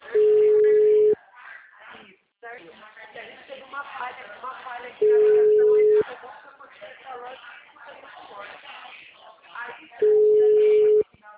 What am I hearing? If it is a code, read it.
Ringback tone (Europe, 425 Hz), with speech.